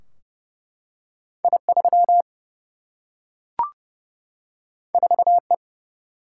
Morse code, heard as a keyed tone, about 30 wpm, 700 Hz. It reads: I 3 4 E
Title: Single-Letter Single-Number 30wpm